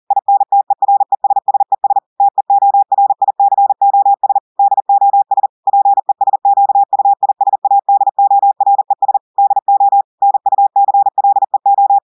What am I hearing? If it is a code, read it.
Morse code, heard as a keyed tone, about 40 wpm, 800 Hz. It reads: INTERESSES TEORICOS DOS PESQUISADORES DO NUCLEO
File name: audios / portugais / portugais texte en clair 1 vitesse 40 mots minutes